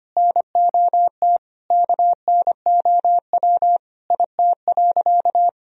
Morse code, heard as a keyed tone, about 25 wpm, 700 Hz. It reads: NOT KNOW IT.
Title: Sentences from Top 1000 Words - Review Part 1 - 25wpm